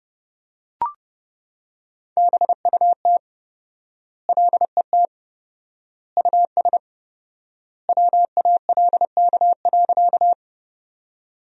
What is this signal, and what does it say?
Morse code, heard as a keyed tone, about 30 wpm, 700 Hz. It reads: BUT LET US WALK.
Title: 4x Word-Spacing - Sentences from Top 300 Words - 30wpm